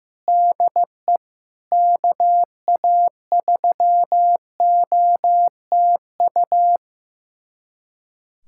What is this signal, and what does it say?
Morse code, heard as a keyed tone, about 15 wpm, 700 Hz. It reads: DE KA3OTU
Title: USA Callsign DE Callsign 15wpm